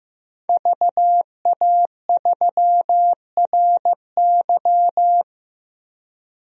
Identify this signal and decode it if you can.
Morse code, heard as a keyed tone, about 15 wpm, 700 Hz. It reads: VA3RY
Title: DXCC-100 Call Signs - 5 Characters - 15wpm